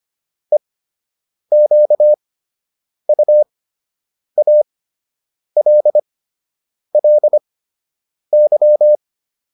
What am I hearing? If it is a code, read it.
Morse code, heard as a keyed tone, about 25 wpm, spaced as for 8 wpm, 600 Hz. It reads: EQUALLY